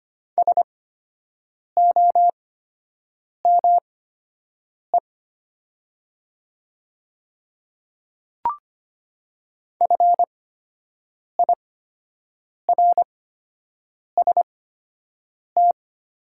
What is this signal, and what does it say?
Morse code, heard as a keyed tone, about 25 wpm, spaced as for 7 wpm, 700 Hz. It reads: SOME FIRST
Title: 8x Character Spacing - Top 100 Words - 25wpm